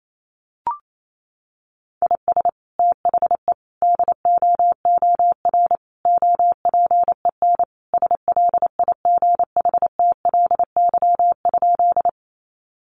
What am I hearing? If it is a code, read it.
Morse code, heard as a keyed tone, about 28 wpm, 700 Hz. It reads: IS THE DOOR OPEN SLIGHTLY?
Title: Sentences from Top 1300 Words 28wpm